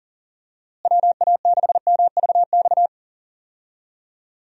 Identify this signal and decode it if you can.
Morse code, heard as a keyed tone, about 40 wpm, 700 Hz. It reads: WA6MVX
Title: USA Call Signs - 6 Characters - Repeated 2x - 40wpm